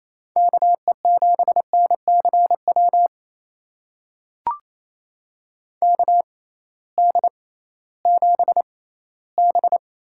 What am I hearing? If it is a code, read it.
Morse code, heard as a keyed tone, about 28 wpm, spaced as for 16 wpm, 700 Hz. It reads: KE7NCW KD7B
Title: USA Call Signs - 6 Characters - 6x Character Spacing - 28wpm